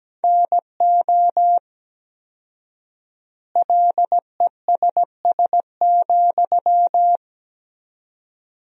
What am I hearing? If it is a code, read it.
Morse code, heard as a keyed tone, about 17 wpm, 700 Hz. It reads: NO LESS,
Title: 4x Word-Spacing - Sentences from Top 400 Words - 17wpm